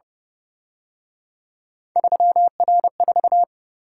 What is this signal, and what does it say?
Morse code, heard as a keyed tone, about 29 wpm, 700 Hz. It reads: 3R4
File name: Speed-Racing - Three Letter-Number Combinations - 25wpm